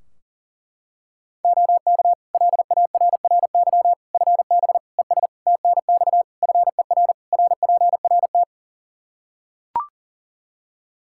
Morse code, heard as a keyed tone, about 40 wpm, 700 Hz. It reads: OK LARRY FB ES TNX FER RPRT
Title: Practice QSO - Exchange 2 - Info - 40wpm